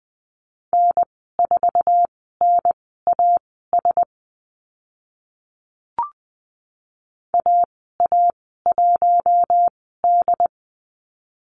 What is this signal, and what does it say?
Morse code, heard as a keyed tone, about 20 wpm, spaced as for 14 wpm, 700 Hz. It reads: N4NAS AA1D
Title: 076 Lesson 13g - Letter L - Call Signs 20wpm